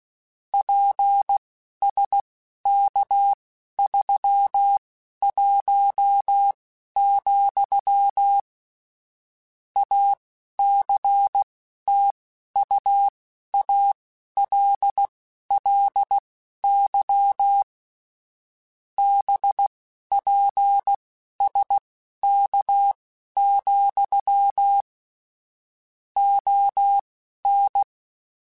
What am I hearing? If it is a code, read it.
Morse code, heard as a keyed tone, about 16 wpm, spaced as for 11 wpm, 800 Hz. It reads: PSK31, ACTUALLY BPSK, ON
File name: QSOs / QSOs-mp3s / kc7haa@10